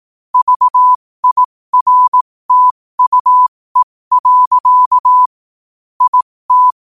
Morse code, heard as a keyed tone, about 18 wpm, spaced as for 15 wpm, 1000 Hz. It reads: VIRTUE. IT
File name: qotdfiles / 15wpm